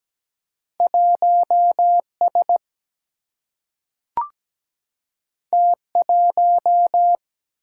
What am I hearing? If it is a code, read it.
Morse code, heard as a keyed tone, about 17 wpm, 700 Hz. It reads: 1 S T 1
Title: Single-Letter Single-Number 17wpm